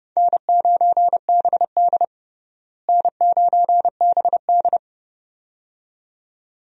Morse code, heard as a keyed tone, about 30 wpm, 700 Hz. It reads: N9BD N9BD
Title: USA Call Signs - 4 Characters - Repeated 2x - 30wpm